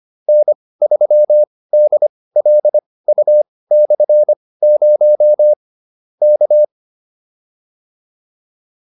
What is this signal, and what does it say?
Morse code, heard as a keyed tone, about 25 wpm, spaced as for 18 wpm, 600 Hz. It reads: N3DLU/0 K